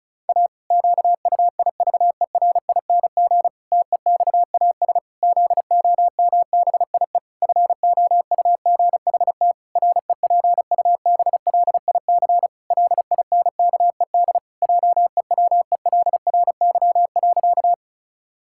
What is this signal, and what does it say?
Morse code, heard as a keyed tone, about 34 wpm, 700 Hz. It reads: A QUIVERING TEXAS ZOMBIE FOUGHT REPUBLIC LINKED JEWELRY.
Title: Pangram Challenge v2 - 35wpm